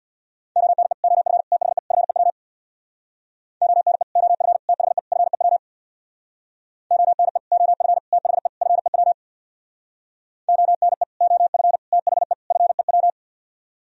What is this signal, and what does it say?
Morse code, heard as a keyed tone, about 64 wpm, 700 Hz. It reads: ONE OF THE FEW ONE OF THE FEW ONE OF THE FEW ONE OF THE FEW